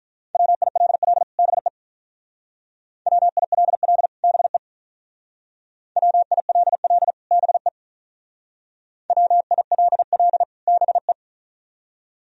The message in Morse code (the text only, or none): WILL BE WILL BE WILL BE WILL BE